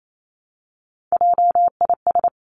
Morse code, heard as a keyed tone, about 28 wpm, 700 Hz. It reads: JIS
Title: Three Challenging Letter-Number Combinations 28wpm